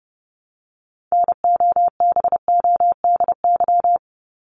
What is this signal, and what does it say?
Morse code, heard as a keyed tone, about 30 wpm, 700 Hz. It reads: NOBODY
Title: Top 1501-1750 Words 30wpm